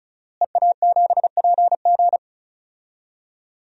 Morse code, heard as a keyed tone, about 36 wpm, 700 Hz. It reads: EA7PG
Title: DXCC-100 Call Signs - 5 Characters - Repeated 3x - 35wpm